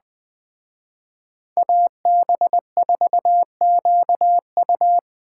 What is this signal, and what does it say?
Morse code, heard as a keyed tone, about 20 wpm, 700 Hz. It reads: AB4QU
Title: Speed Racing - USA 2x2 Call Signs 17wpm